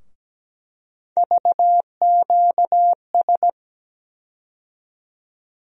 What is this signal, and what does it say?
Morse code, heard as a keyed tone, about 17 wpm, 700 Hz. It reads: VQS
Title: USA Call Sign Postfixes - 3 Letters 17wpm